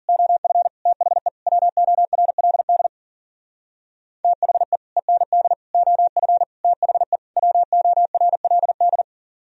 Morse code, heard as a keyed tone, about 52 wpm, 700 Hz. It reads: OF THE WORLD THE END OF THE WORLD